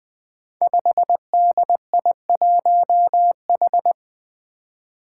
Morse code, heard as a keyed tone, about 20 wpm, 700 Hz. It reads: 5DI1H
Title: Five Challenging Letter-Number Combinations 20wpm